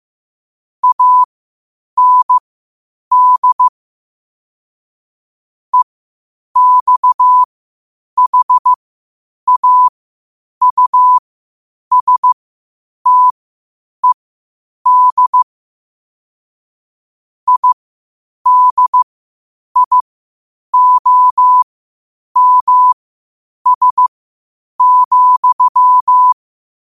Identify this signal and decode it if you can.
Morse code, heard as a keyed tone, about 15 wpm, spaced as for 8 wpm, 1000 Hz. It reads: AND EXHAUSTED IDIOMS,
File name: qotdfiles / 8wpm